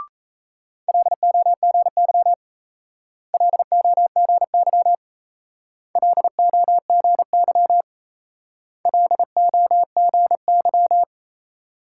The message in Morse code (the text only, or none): LOGY LOGY LOGY LOGY